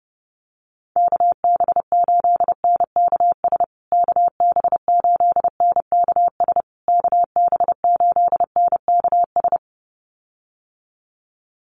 Morse code, heard as a keyed tone, about 30 wpm, 700 Hz. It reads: KB8NKS KB8NKS KB8NKS
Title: USA Call Signs Repeated 3x 30wpm